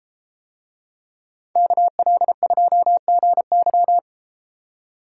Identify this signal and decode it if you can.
Morse code, heard as a keyed tone, about 32 wpm, 700 Hz. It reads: KL2GY